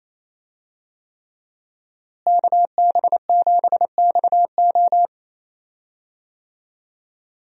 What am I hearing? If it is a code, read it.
Morse code, heard as a keyed tone, about 27 wpm, 700 Hz. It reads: KB7XO
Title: USA Call Signs - 5 Characters - Repeated 3x - 28wpm